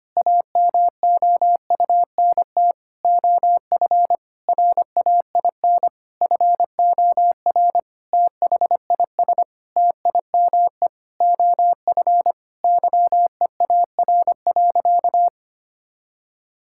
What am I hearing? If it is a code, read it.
Morse code, heard as a keyed tone, about 25 wpm, 700 Hz. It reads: AMOUNT OF RAIN FOR THIS TIME OF YEAR.
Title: Sentences from Top 1000 Words - Review Part 6 - 25wpm